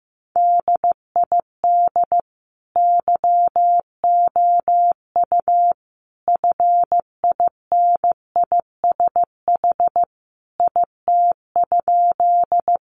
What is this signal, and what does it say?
Morse code, heard as a keyed tone, about 15 wpm, 700 Hz. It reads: DID YOU FINISH IT?
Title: Sentences from Top 1200 Words 15wpm